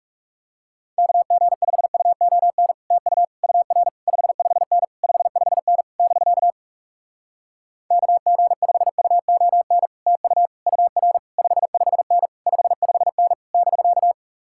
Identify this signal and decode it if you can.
Morse code, heard as a keyed tone, about 40 wpm, 700 Hz. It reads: KG5UON TU UR 55N 55N <BK> KG5UON TU UR 55N 55N <BK>